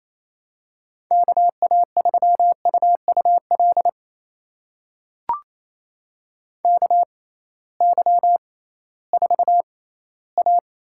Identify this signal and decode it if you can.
Morse code, heard as a keyed tone, about 28 wpm, spaced as for 16 wpm, 700 Hz. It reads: KA3UUL KY4A
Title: USA Call Signs - 6x Character Spacing - 28wpm